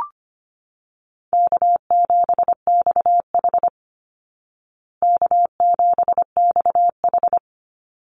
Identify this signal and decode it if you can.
Morse code, heard as a keyed tone, about 25 wpm, 700 Hz. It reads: K7XH K7XH
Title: Call Signs Repeated 3x - 25wpm